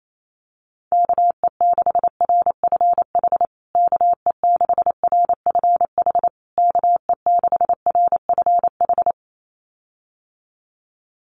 Morse code, heard as a keyed tone, about 28 wpm, 700 Hz. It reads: KE6RFH KE6RFH KE6RFH
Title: USA Call Signs Repeated 3x 28wpm